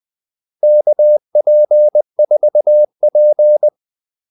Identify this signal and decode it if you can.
Morse code, heard as a keyed tone, about 20 wpm, 600 Hz. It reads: KP4P
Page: Kaldesignaler 20 wpm